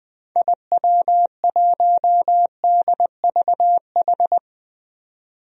Morse code, heard as a keyed tone, about 20 wpm, 700 Hz. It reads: IW1DVH
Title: DXCC-100 Call Signs - 6 Characters - 20wpm